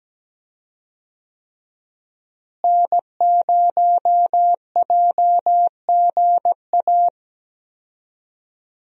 Morse code, heard as a keyed tone, about 17 wpm, 700 Hz. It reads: N0JGA
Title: USA Call Signs - 5 Characters - Repeated 3x - 17wpm